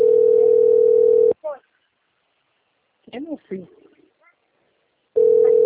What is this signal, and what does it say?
Ringback tone (North America), 440 + 480 Hz, with speech.
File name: rg-1600-unknown- - - .10226